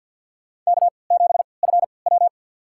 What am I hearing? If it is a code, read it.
Morse code, heard as a keyed tone, about 46 wpm, spaced as for 27 wpm, 700 Hz. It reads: K7FW
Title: USA Call Signs - 4 Characters - 3x Character Spacing - 50wpm